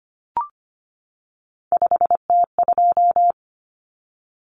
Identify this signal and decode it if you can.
Morse code, heard as a keyed tone, about 25 wpm, 700 Hz. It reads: 5T2
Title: Common Cut Numbers - 3-Digit 25wpm